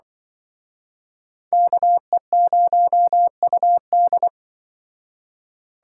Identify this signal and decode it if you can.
Morse code, heard as a keyed tone, about 24 wpm, 700 Hz. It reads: KE0UD
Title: Speed Racing - USA 2x2 Call Signs 20wpm